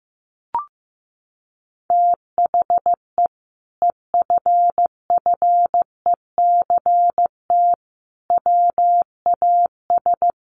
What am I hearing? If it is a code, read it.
Morse code, heard as a keyed tone, about 15 wpm, 700 Hz. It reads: THE EFFECT WAS
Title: Sentences from Top 1200 Words 15wpm